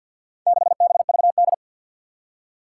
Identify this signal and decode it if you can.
Morse code, heard as a keyed tone, about 52 wpm, 700 Hz. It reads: 6BVD
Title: Four Challenging Letter-Number Combinations 50wpm